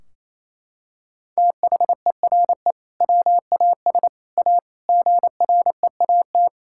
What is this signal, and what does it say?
Morse code, heard as a keyed tone, about 28 wpm, 700 Hz. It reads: THERE WAS A GREAT
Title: Sentences from Top 1600 Words 28wpm